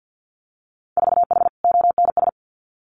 Morse code, heard as a keyed tone, about 50 wpm, 700 Hz. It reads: 4H ONS
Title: Winter Field Day Exchanges Encore 50wpm